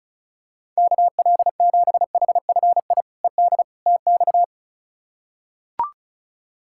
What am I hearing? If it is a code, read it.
Morse code, heard as a keyed tone, about 36 wpm, 700 Hz. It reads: KL7HFI ED TX
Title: K1USN SST QSO Practice - 35wpm